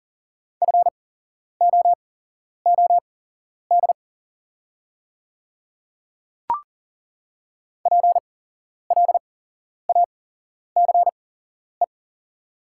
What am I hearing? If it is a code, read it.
Morse code, heard as a keyed tone, about 40 wpm, spaced as for 11 wpm, 700 Hz. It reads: FOOD PLACE
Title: 8x Character Spacing - Top 300 Words - 40wpm